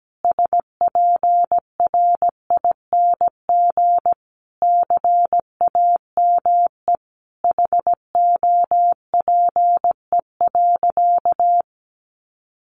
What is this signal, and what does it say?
Morse code, heard as a keyed tone, about 17 wpm, 700 Hz. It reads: SPRING CAME HOPE.
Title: Sentences from Top 1200 Words 17wpm